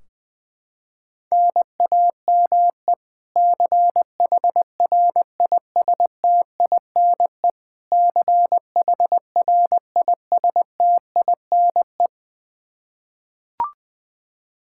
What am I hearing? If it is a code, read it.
Morse code, heard as a keyed tone, about 20 wpm, 700 Hz. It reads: NAME CHRISTINE CHRISTINE
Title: Practice QSO - Exchange 1 - Beginning - 20wpm